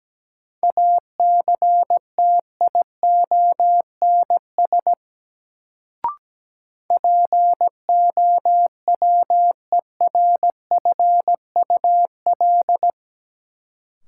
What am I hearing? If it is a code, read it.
Morse code, heard as a keyed tone, about 17 wpm, 700 Hz. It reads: ACTIONS POWERFUL
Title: Sentences from Top 1000 Words - Review Part 6 - New Word Drill - 17wpm